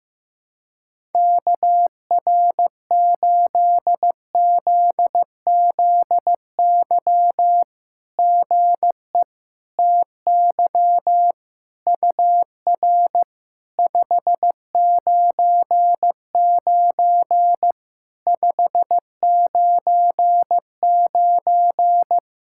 Morse code, heard as a keyed tone, about 15 wpm, 700 Hz. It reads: KR8ZZY GE TY UR 599 599
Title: SOTA QSO Practice 15wpm